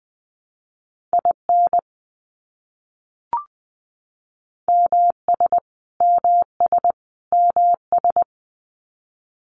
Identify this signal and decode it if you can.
Morse code, heard as a keyed tone, about 20 wpm, 700 Hz. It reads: IN MS MS MS
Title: US States Repeated 3x 20wpm